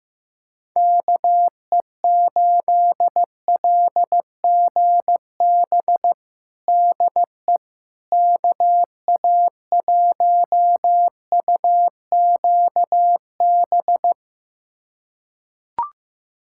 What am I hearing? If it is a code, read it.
Morse code, heard as a keyed tone, about 15 wpm, 700 Hz. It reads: KE8LGB DE KA1UQB
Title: USA Callsign DE Callsign - 6 Characters - 15wpm